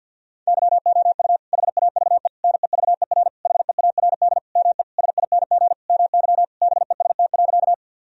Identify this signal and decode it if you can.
Morse code, heard as a keyed tone, about 50 wpm, 700 Hz. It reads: YOU HAVE NEVER HEARD ME SING MY BEST.
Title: Sentences from Top 1000 Words - Review Part 2 - 50wpm